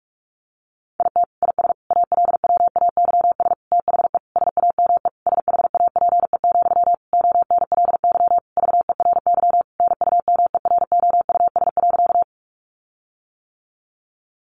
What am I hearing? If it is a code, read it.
Morse code, heard as a keyed tone, about 44 wpm, 700 Hz. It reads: IT IS ALWAYS THE SAME SHAPE, ONLY VERY NUMEROUS.
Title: Sentences from Top 1400 Words 45wpm